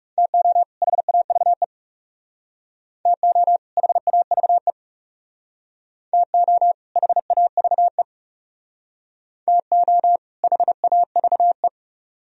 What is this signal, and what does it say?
Morse code, heard as a keyed tone, about 40 wpm, 700 Hz. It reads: TO HAVE TO HAVE TO HAVE TO HAVE